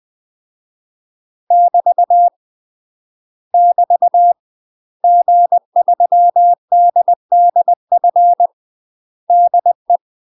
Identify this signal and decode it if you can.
Morse code, heard as a keyed tone, about 20 wpm, 700 Hz. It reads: <BT> <BT> G3DDF DE
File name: UK MASTER QSOs TRAINING / 20WPM / UK MASTER QSOs 0072